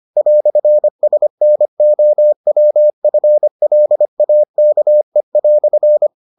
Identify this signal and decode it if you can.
Morse code, heard as a keyed tone, about 25 wpm, 600 Hz. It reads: "SNOWFLAKE"